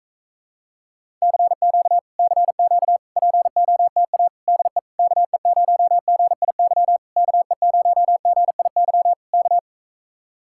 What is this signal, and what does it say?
Morse code, heard as a keyed tone, about 45 wpm, 700 Hz. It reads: CQ CQ POTA DE KE0GIY KE0GIY K